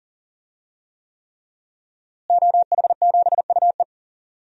Morse code, heard as a keyed tone, about 37 wpm, 700 Hz. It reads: OH7UE
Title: DXCC-100 Call Signs - 5 Characters - Repeated 3x - 40wpm